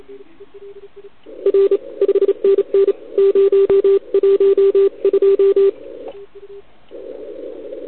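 Morse code, heard as a keyed tone, about 28 wpm, 390 Hz, with radio noise. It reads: R 5NN 012
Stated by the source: CW from the OKOM DX contest on 80 m, received with a TS480.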